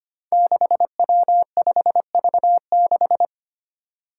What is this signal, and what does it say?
Morse code, heard as a keyed tone, about 25 wpm, 700 Hz. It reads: 6W5V6
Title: Five Challenging Letter-Number Combinations 25wpm